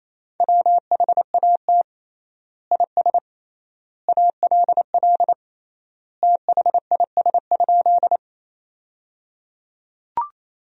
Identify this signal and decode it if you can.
Morse code, heard as a keyed tone, about 28 wpm, 700 Hz. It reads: WHAT IS ALL THIS?
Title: 3x Word-Spacing - Sentences from Top 100 Words - 28wpm